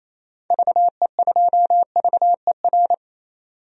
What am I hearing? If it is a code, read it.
Morse code, heard as a keyed tone, about 28 wpm, 700 Hz. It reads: VE2VER
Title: DXCC-100 Call Signs - 6 Characters - 28wpm